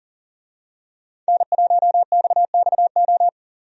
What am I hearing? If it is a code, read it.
Morse code, heard as a keyed tone, about 39 wpm, 700 Hz. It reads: N1XXO
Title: USA Call Signs - 5 Characters - 40wpm